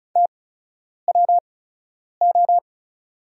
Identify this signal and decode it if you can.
Morse code, heard as a keyed tone, about 36 wpm, spaced as for 10 wpm, 690 Hz. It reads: TWO
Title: 8x Character Spacing - Top 500 Words - 35wpm